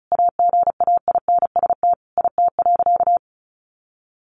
Morse code, heard as a keyed tone, about 34 wpm, 700 Hz. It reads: AGAINST IT.
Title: Sentences from Top 1400 Words 35wpm